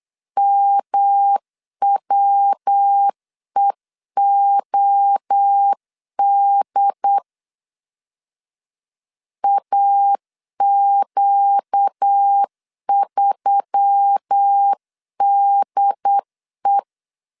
Morse code, heard as a keyed tone, about 8 wpm, 800 Hz. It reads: MWEOD AQ3DE
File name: CW / DET / 1 6 1500C S C 8wpm 5219